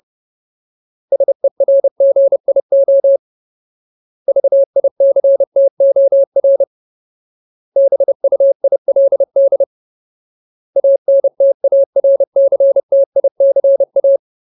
Morse code, heard as a keyed tone, about 30 wpm, 550 Hz. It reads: SERGIO VICTOR BUILD ANTARCTICA
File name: Ultimate Word List / 30WPM Ultimate